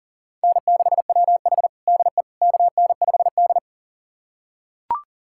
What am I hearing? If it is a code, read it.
Morse code, heard as a keyed tone, about 40 wpm, 700 Hz. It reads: N6WH DE KN5D
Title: USA Callsign DE Callsign - 4 Characters - 40wpm